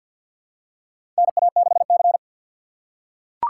Morse code, heard as a keyed tone, about 47 wpm, 700 Hz. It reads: NA6C
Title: USA Call Signs - 4 Characters - 6x Character Spacing - 50wpm